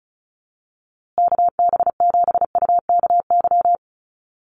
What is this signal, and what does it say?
Morse code, heard as a keyed tone, about 34 wpm, 700 Hz. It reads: KB7UKY
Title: USA Call Signs Repeated 2x 35wpm